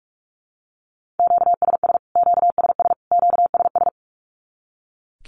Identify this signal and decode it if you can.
Morse code, heard as a keyed tone, about 44 wpm, 700 Hz. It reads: QSS QSS QSS
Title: Q-Codes Repeated 3x 45wpm